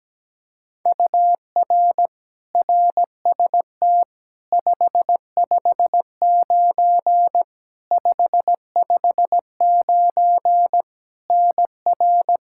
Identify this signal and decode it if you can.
Morse code, heard as a keyed tone, about 17 wpm, 700 Hz. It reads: UR RST 559 559 NR
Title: Practice QSO - Exchange 1 - Beginning - 17wpm